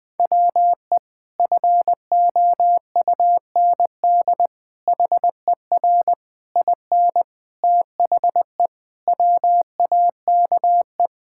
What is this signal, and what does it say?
Morse code, heard as a keyed tone, about 20 wpm, 700 Hz. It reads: WE FOUND HER IN THE WAKE
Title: Sentences with 4-Word Phrases - 20wpm